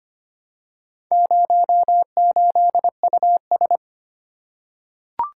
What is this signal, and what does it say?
Morse code, heard as a keyed tone, about 25 wpm, 700 Hz. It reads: 08US
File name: practice / 25wpm / Four Letter-Number Combinations - 25wpm